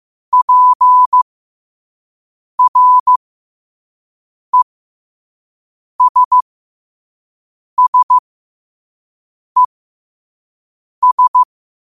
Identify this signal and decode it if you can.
Morse code, heard as a keyed tone, about 15 wpm, spaced as for 5 wpm, 1000 Hz. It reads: PRESSES
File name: qotdfiles / 5wpm